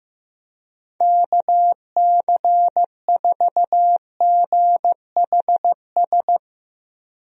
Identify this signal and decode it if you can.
Morse code, heard as a keyed tone, about 15 wpm, 700 Hz. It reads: KC4GHS
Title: USA Call Signs - 6 Characters - Repeated 2x - 15wpm